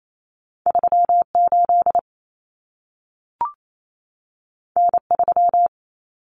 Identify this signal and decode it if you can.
Morse code, heard as a keyed tone, about 28 wpm, 700 Hz. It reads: 38 N3
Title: Common Cut Numbers - 2-Digit 28wpm